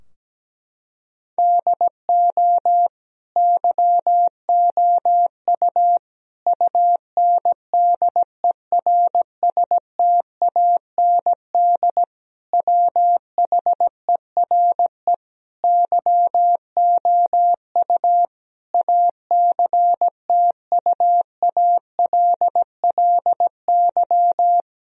Morse code, heard as a keyed tone, about 17 wpm, 700 Hz. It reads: DO YOU UNDERSTAND WHERE YOU ACTUALLY
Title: Sentences from Top 1000 Words - Review Part 4 - 17wpm